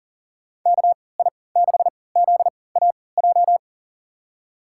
Morse code, heard as a keyed tone, about 40 wpm, spaced as for 23 wpm, 700 Hz. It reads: KI6ZAJ
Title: USA Call Signs - 6 Characters - 3x Character Spacing - 40wpm